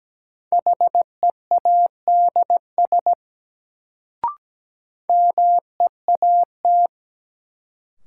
Morse code, heard as a keyed tone, about 17 wpm, 700 Hz. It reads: HEADS MEAT
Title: Sentences from Top 1000 Words - Review Part 3 - New Word Drill - 17wpm